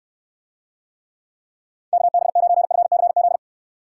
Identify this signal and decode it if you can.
Morse code, heard as a keyed tone, about 69 wpm, 700 Hz. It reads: KB9FCZ